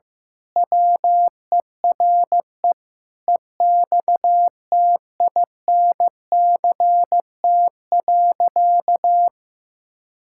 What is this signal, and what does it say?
Morse code, heard as a keyed tone, about 15 wpm, 700 Hz. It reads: WERE EXTINCT.
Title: Pangram Challenge v2 - 15wpm